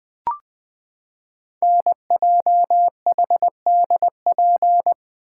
Morse code, heard as a keyed tone, about 20 wpm, 700 Hz. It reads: NJHDP
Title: Code Groups - 5 Letters - 20wpm